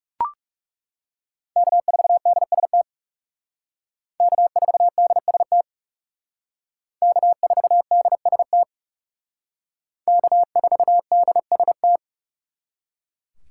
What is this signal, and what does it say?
Morse code, heard as a keyed tone, about 38 wpm, 700 Hz. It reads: K4DST K4DST K4DST K4DST
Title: Speed Racing - USA Call Signs 30wpm